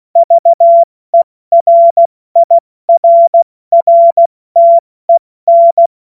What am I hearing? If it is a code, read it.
Morse code, heard as a keyed tone, about 16 wpm, 680 Hz. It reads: VERIRRTEN
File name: Morsekurs / Witze / Witz 03. Mai 2022 70 F80